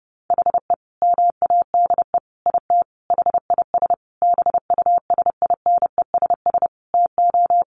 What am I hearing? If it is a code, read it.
Morse code, heard as a keyed tone, about 30 wpm, 700 Hz. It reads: HE MADE IT HIS BUSINESS TO
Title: Sentences from Top 1400 Words 30wpm